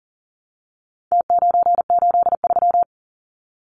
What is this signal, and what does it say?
Morse code, heard as a keyed tone, about 40 wpm, 700 Hz. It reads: T983
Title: Numbers - Contest Style - 4-Digit 40wpm